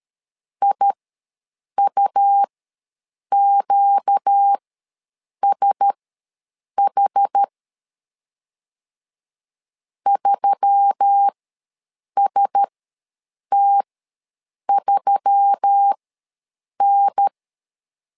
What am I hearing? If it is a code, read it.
Morse code, heard as a keyed tone, about 13 wpm, spaced as for 7 wpm, 800 Hz. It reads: IUQSH 3ST3N
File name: LEZIONI / 4 Q1S3 ws 12wpm 2038